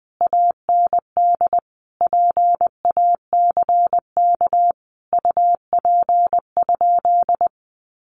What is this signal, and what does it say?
Morse code, heard as a keyed tone, about 20 wpm, 700 Hz. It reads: AND PACK UP?
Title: Sentences from Top 1300 Words 20wpm